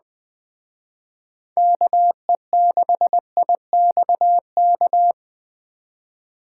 Morse code, heard as a keyed tone, about 21 wpm, 700 Hz. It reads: KE6IXK